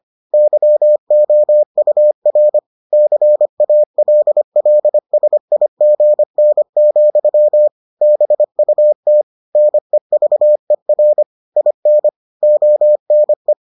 Morse code, heard as a keyed tone, about 25 wpm, 600 Hz. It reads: YOUR CALLSIGN, BUT NEVER IN ONE